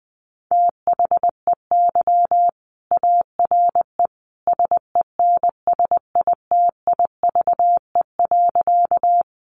Morse code, heard as a keyed tone, about 20 wpm, 700 Hz. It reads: THEY ARE SENSITIVE.
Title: Sentences from Top 1400 Words 20wpm